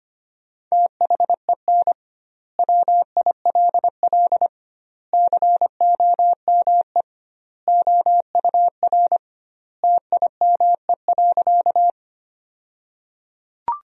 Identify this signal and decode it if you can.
Morse code, heard as a keyed tone, about 25 wpm, 700 Hz. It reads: THEN WILL COME OUR TIME.
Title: 2x Word-Spacing - Sentences from Top 200 Words - 25wpm